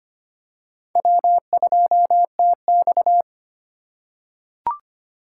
Morse code, heard as a keyed tone, about 25 wpm, 700 Hz. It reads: W2TX
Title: USA Call Signs - 4 Characters - 3x Character Spacing - 25wpm